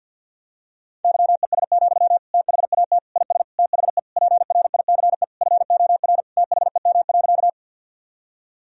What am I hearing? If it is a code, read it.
Morse code, heard as a keyed tone, about 51 wpm, 700 Hz. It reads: YES, THAT IS THE PRICE FOR THEM.
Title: Sentences from Top 1000 Words - Review Part 4 - 50wpm